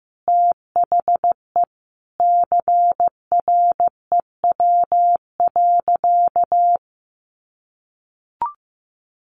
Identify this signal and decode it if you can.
Morse code, heard as a keyed tone, about 15 wpm, 700 Hz. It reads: THE CREW.
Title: Sentences from Top 1500 Words 15wpm